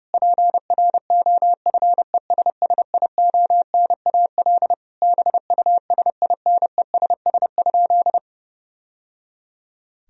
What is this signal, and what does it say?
Morse code, heard as a keyed tone, about 30 wpm, 700 Hz. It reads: PROFESSIONAL BUSINESS?
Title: Sentences from Top 1000 Words - Review Part 4 - 30wpm